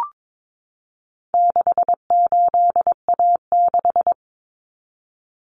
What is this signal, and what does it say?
Morse code, heard as a keyed tone, about 22 wpm, 700 Hz. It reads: 68A6
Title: Common Cut Numbers - 4-Digit 22wpm